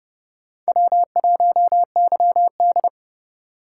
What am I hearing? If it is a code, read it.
Morse code, heard as a keyed tone, about 29 wpm, 700 Hz. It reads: W1YD